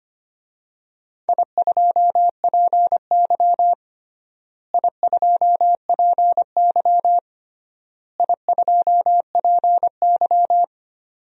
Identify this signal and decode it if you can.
Morse code, heard as a keyed tone, about 25 wpm, 700 Hz. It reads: I2PY I2PY I2PY
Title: DXCC-100 Call Signs - 4 Characters - Repeated 3x - 25wpm